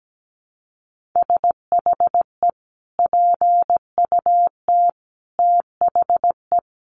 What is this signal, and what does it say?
Morse code, heard as a keyed tone, about 17 wpm, 700 Hz. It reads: SHE PUT THE
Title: Sentences from Top 1400 Words 17wpm